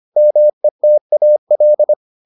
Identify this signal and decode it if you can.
Morse code, heard as a keyed tone, about 25 wpm, 600 Hz. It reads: METAL